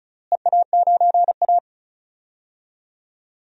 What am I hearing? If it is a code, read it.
Morse code, heard as a keyed tone, about 36 wpm, 700 Hz. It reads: EA9A
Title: DXCC-100 Call Signs - 4 Characters - Repeated 2x - 35wpm